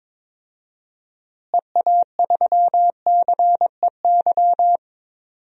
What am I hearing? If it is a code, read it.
Morse code, heard as a keyed tone, about 21 wpm, 700 Hz. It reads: EA3CEY
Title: DXCC-100 Call Signs - 6 Characters - Repeated 2x - 22wpm